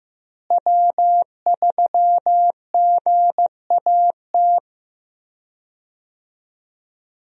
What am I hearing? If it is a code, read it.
Morse code, heard as a keyed tone, about 15 wpm, 700 Hz. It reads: W3GAT
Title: USA Call Signs - 5 Characters - Repeated 3x - 15wpm